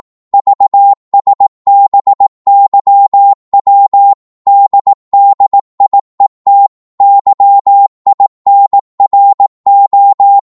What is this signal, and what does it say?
Morse code, heard as a keyed tone, about 18 wpm, 800 Hz. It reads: VSBYW DDIET YINRO
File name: Morse / ltr-18wpm